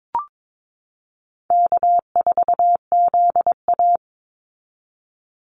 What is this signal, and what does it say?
Morse code, heard as a keyed tone, about 22 wpm, 700 Hz. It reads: K4ZA
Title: Easy Call Signs Repeated 3x - 22wpm